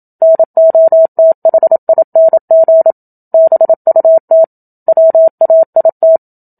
Morse code, heard as a keyed tone, about 27 wpm, 650 Hz. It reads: NOTHING BUT WAIT
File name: Book Two / 01-Chapter 1 - 27 WPM